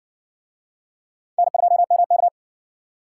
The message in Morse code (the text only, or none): N2MK